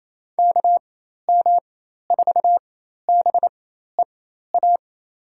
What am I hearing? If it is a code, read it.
Morse code, heard as a keyed tone, about 28 wpm, spaced as for 13 wpm, 690 Hz. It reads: KM4BEA
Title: USA Call Signs - 6 Characters - 4x Character Spacing - 28wpm